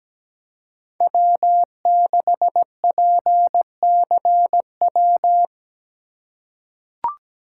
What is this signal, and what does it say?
Morse code, heard as a keyed tone, about 17 wpm, 700 Hz. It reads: W6PCW
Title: USA Call Signs - 5 Characters - 3x Character Spacing - 17wpm